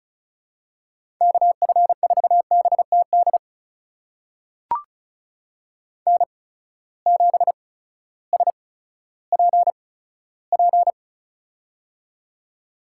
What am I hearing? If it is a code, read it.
Morse code, heard as a keyed tone, about 36 wpm, spaced as for 16 wpm, 700 Hz. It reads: KF4BTD N7SPP
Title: USA Call Signs - 8x Character Spacing - 35wpm